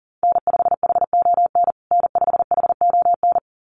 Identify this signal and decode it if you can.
Morse code, heard as a keyed tone, about 37 wpm, 700 Hz. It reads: N5HON N5HON
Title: USA Call Signs Repeated 2x 40wpm